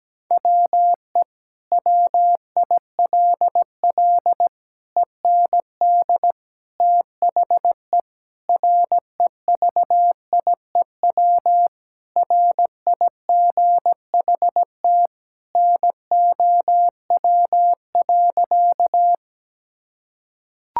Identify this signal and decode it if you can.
Morse code, heard as a keyed tone, about 17 wpm, 700 Hz. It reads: WE WILL END THE REVIEW RIGHT NOW.
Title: Sentences from Top 1000 Words - Review Part 6 - 17wpm